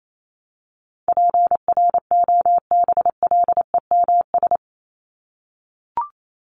Morse code, heard as a keyed tone, about 28 wpm, 700 Hz. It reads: PROBLEMS
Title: ES:S Suffix - Top 100 Words with Base Words 28wpm